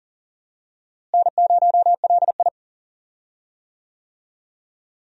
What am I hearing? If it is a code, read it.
Morse code, heard as a keyed tone, about 40 wpm, 700 Hz. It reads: N0LI
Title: USA Call Signs - 4 Characters - Repeated 3x - 40wpm